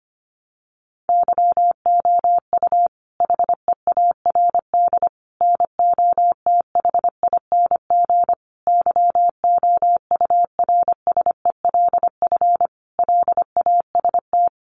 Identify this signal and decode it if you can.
Morse code, heard as a keyed tone, about 25 wpm, 700 Hz. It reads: YOU HEARD NOTHING YOURSELF LAST
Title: Sentences from Top 600 Words 25wpm